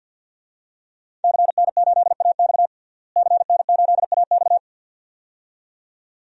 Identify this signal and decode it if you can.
Morse code, heard as a keyed tone, about 49 wpm, 700 Hz. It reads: CN8AX CN8AX
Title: DXCC-100 Call Signs - 5 Characters - Repeated 2x - 50wpm